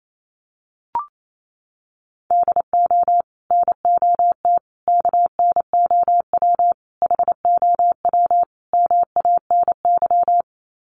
Morse code, heard as a keyed tone, about 28 wpm, 700 Hz. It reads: DO NOT KNOW HOW MANY
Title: Sets of 5 Words Encore 28wpm